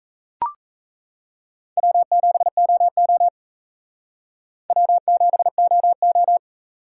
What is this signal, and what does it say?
Morse code, heard as a keyed tone, about 34 wpm, 700 Hz. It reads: W7OO W7OO
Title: Speed Racing - USA 1x2 Call Signs 28wpm